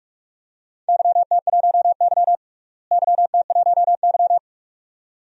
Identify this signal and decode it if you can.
Morse code, heard as a keyed tone, about 46 wpm, 700 Hz. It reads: YT1Y YT1Y
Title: DXCC-100 Call Signs - 4 Characters - Repeated 2x - 45wpm